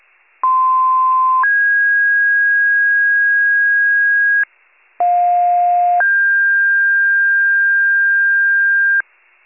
Two-tone page, heard as a one-second tone, then a three-second tone, about 1040 then 1650 Hz, about 710 then 1600 Hz. It is heard from 0.4 to 9.0 s.